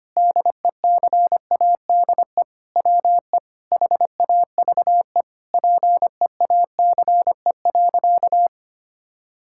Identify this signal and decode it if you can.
Morse code, heard as a keyed tone, about 25 wpm, 700 Hz. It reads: DECADE WE HAVE PEACE.
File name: practice / 25wpm / Sentences with 9-Word Phrases - 25wpm